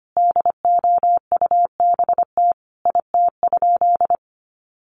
Morse code, heard as a keyed tone, about 25 wpm, 700 Hz. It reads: DOUBT IT?
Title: Sentences from Top 1500 Words 25wpm